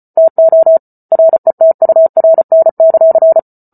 Morse code, heard as a keyed tone, about 34 wpm, 650 Hz. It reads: TO RETURN;